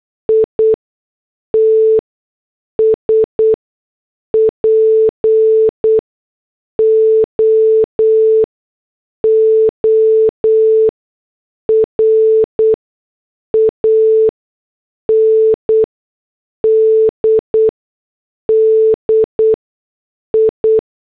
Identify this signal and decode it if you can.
Morse code, heard as a keyed tone, about 8 wpm, spaced as for 6 wpm, 440 Hz. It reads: ITS POOR AND DI
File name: morse / headlines